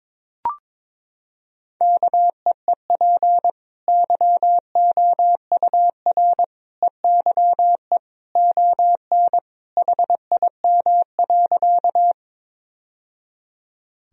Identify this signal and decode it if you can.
Morse code, heard as a keyed tone, about 22 wpm, 700 Hz. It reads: KEEP YOUR EYE ON HIM.
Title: Sentences from Top 1000 Words - Review Part 1 - 22wpm